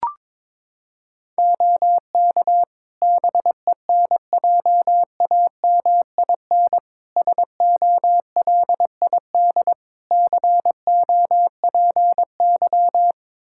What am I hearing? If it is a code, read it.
Morse code, heard as a keyed tone, about 22 wpm, 700 Hz. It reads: OK BENJAMIN SOLID COPY
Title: K1USN SST QSO Practice 22wpm